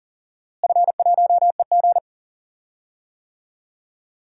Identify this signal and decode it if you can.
Morse code, heard as a keyed tone, about 38 wpm, 700 Hz. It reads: F1EG
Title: DXCC-100 Call Signs - 4 Characters - Repeated 2x - 40wpm